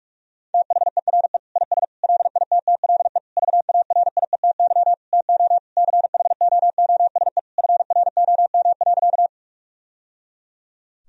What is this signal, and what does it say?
Morse code, heard as a keyed tone, about 46 wpm, 700 Hz. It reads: THERE IS LITTLE VARIETY TO CHOOSE FROM.
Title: Sentences from Top 1000 Words - Review Part 4 - 45wpm